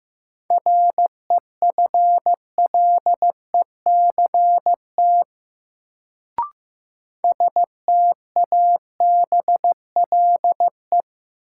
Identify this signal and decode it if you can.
Morse code, heard as a keyed tone, about 15 wpm, 700 Hz. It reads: REFLECT STABLE
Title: Sentences from Top 1500 Words - New Word Drill Alt - 15wpm